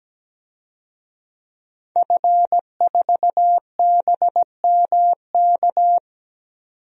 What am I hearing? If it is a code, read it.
Morse code, heard as a keyed tone, about 17 wpm, 700 Hz. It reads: F4BMK
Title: DXCC-100 Call Signs - 5 Characters - 17wpm